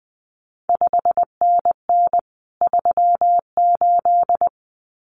Morse code, heard as a keyed tone, about 20 wpm, 700 Hz. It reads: 5NN 38
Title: CQ WW DX Contest Exchange Encore - 20wpm